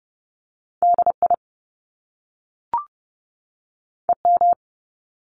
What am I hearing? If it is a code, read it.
Morse code, heard as a keyed tone, about 30 wpm, 700 Hz. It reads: DI EM
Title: Top Two Letter Combinations in Words 30wpm